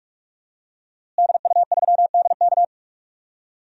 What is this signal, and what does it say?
Morse code, heard as a keyed tone, about 46 wpm, 700 Hz. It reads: DU3DK
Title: DXCC-100 Call Signs - 5 Characters - Repeated 3x - 45wpm